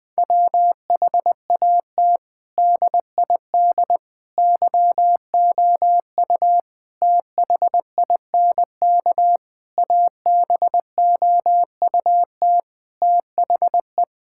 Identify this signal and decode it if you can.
Morse code, heard as a keyed tone, about 21 wpm, 700 Hz. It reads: WHAT DID YOU THINK ABOUT THE
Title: Sentences from Top 1000 Words - Review Part 5 - 20wpm